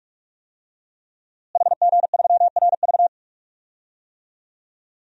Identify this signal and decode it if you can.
Morse code, heard as a keyed tone, about 41 wpm, 700 Hz. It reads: HG3RV
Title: DXCC-100 Call Signs - 5 Characters - Repeated 2x - 45wpm